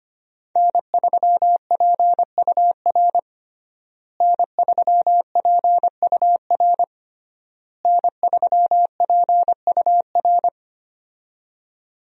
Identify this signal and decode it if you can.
Morse code, heard as a keyed tone, about 25 wpm, 700 Hz. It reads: N3PUR N3PUR N3PUR
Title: USA Call Signs - 5 Characters - Repeated 3x - 25wpm